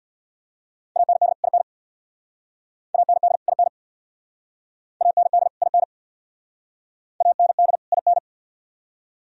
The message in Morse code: AND IN AND IN AND IN AND IN

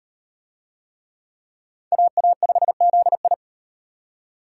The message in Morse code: AA5ZI

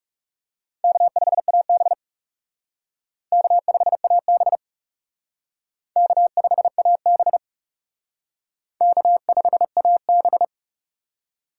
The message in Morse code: K5AB K5AB K5AB K5AB

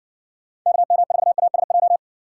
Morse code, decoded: KM4ASJ

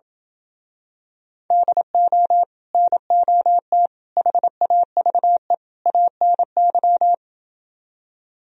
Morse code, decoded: DO NOT HAVE ANY

